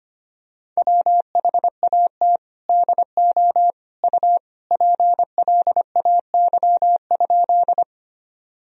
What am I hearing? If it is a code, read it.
Morse code, heard as a keyed tone, about 25 wpm, 700 Hz. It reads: WHAT DO U PLAY?
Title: Sentences from Top 200 Words and Abbreviations 25wpm